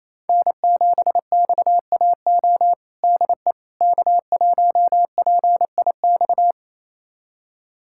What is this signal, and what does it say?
Morse code, heard as a keyed tone, about 28 wpm, 700 Hz. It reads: N7XAO DE K1PIX
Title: USA Callsign DE Callsign - 5 Characters - 28wpm